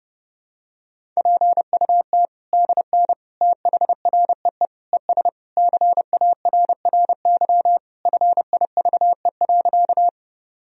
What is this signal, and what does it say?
Morse code, heard as a keyed tone, about 30 wpm, 700 Hz. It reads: PUT DN THREE ES CARRY FIVE.
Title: Sentences from Top 400 Words and Abbreviations 30wpm